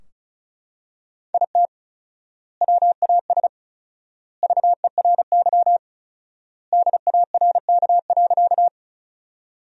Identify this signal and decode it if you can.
Morse code, heard as a keyed tone, about 34 wpm, 700 Hz. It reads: IT WAS VERY DARK.
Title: 4x Word-Spacing - Sentences from Top 500 Words - 35wpm